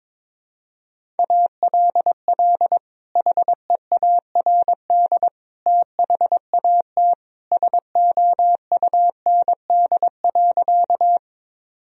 Morse code, heard as a keyed tone, about 22 wpm, 700 Hz. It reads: ALL HEARD THAT SOUND.
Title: Sentences from Top 1000 Words - Review Part 1 - 22wpm